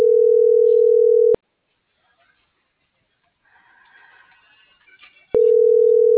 Ringback tone (North America), 440 + 480 Hz.